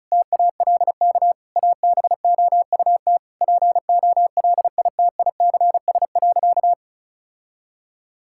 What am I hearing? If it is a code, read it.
Morse code, heard as a keyed tone, about 36 wpm, 700 Hz. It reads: TALK ABOUT POLITICS.
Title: Sentences from Top 1000 Words - Review Part 6 - 35wpm